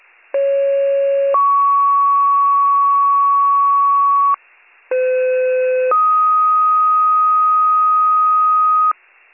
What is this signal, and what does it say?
Two-tone page, heard as a one-second tone, then a three-second tone, about 550 then 1120 Hz, about 510 then 1250 Hz. It is heard from 0.3 to 8.9 s.